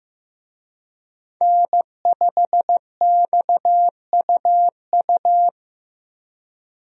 Morse code, heard as a keyed tone, about 15 wpm, 700 Hz. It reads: N5XUU
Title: USA Call Signs - 5 Characters - Repeated 2x - 15wpm